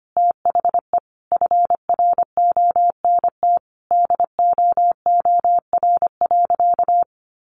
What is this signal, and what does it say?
Morse code, heard as a keyed tone, about 25 wpm, 700 Hz. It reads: THE FRONT DOOR.
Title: Sentences from Top 1300 Words 25wpm